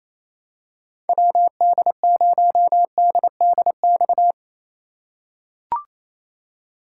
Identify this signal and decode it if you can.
Morse code, heard as a keyed tone, about 28 wpm, 700 Hz. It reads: WD0DDX
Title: USA Call Signs - 3x Character Spacing - 28wpm